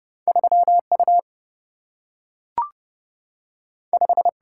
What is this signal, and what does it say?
Morse code, heard as a keyed tone, about 30 wpm, 700 Hz. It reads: 3U 5
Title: Single-Letter Single-Number 30wpm